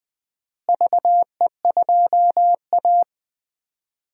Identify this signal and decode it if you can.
Morse code, heard as a keyed tone, about 20 wpm, 700 Hz. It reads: VE2A
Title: DXCC-100 Call Signs - 4 Characters - Repeated 3x - 20wpm